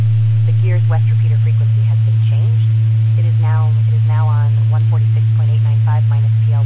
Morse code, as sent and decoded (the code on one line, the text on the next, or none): none